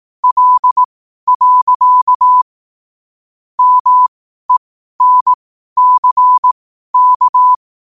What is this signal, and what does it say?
Morse code, heard as a keyed tone, about 18 wpm, spaced as for 12 wpm, 1000 Hz. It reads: L. MENCK
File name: qotdfiles / 12wpm